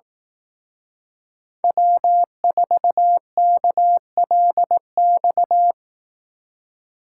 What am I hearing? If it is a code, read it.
Morse code, heard as a keyed tone, about 18 wpm, 700 Hz. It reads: W4KLX
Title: Speed Racing - USA 1x3 Call Signs 15wpm